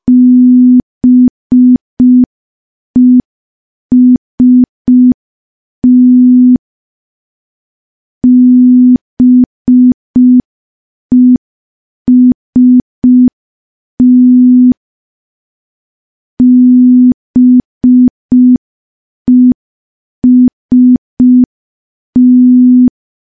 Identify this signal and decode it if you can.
Morse code, heard as a keyed tone, about 5 wpm, 250 Hz. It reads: BEST BEST BEST
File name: morse / morse-training-05WPM